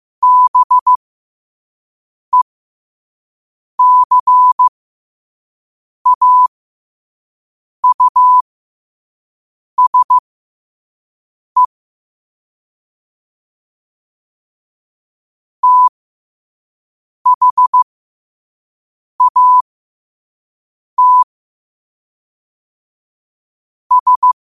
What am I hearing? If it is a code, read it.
Morse code, heard as a keyed tone, about 15 wpm, spaced as for 5 wpm, 1000 Hz. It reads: BECAUSE THAT S
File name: qotdfiles / 5wpm